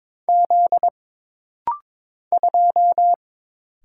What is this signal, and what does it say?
Morse code, heard as a keyed tone, about 22 wpm, 700 Hz. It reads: Z 2
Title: Single Letter-Number - Rapid-Fire 22wpm